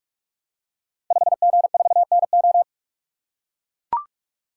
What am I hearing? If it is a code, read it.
Morse code, heard as a keyed tone, about 44 wpm, 700 Hz. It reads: 5G4NO
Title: Code Groups - 5 Letter:Numbers - 45wpm